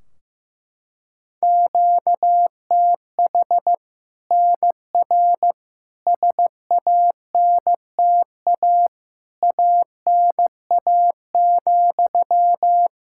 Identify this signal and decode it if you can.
Morse code, heard as a keyed tone, about 15 wpm, 700 Hz. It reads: QTH NR SANTA ANA,
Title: Practice QSO - Exchange 1 - Beginning - 15wpm